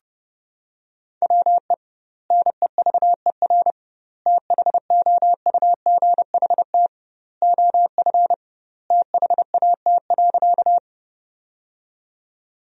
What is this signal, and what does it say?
Morse code, heard as a keyed tone, about 32 wpm, 700 Hz. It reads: WE NEVER THOUGHT OF THAT.
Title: 2x Word-Spacing - Sentences from Top 300 Words - 30wpm